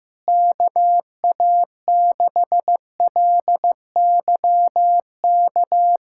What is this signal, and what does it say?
Morse code, heard as a keyed tone, about 15 wpm, 700 Hz. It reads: KA6LYK
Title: USA Callsign DE Callsign 15wpm